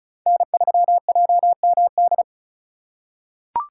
Morse code, heard as a keyed tone, about 36 wpm, 700 Hz. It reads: N3JMD
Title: USA Call Signs - 5 Characters - 3x Character Spacing - 35wpm